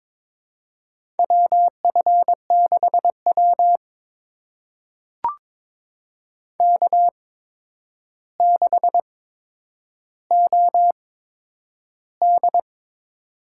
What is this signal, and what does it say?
Morse code, heard as a keyed tone, about 22 wpm, spaced as for 9 wpm, 700 Hz. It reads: WF6W K6OD
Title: USA Call Signs - 4 Characters - 8x Character Spacing - 22wpm